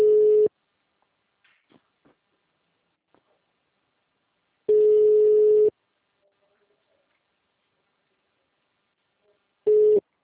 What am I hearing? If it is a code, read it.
Ringback tone (Europe, 425 Hz).